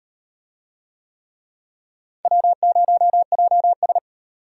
Morse code, heard as a keyed tone, about 36 wpm, 700 Hz. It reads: W0JS